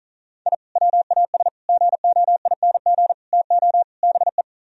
Morse code, heard as a keyed tone, about 43 wpm, 700 Hz. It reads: I WAS GOING TO BE
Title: Speed-Racing - Sets of 5 Words 35wpm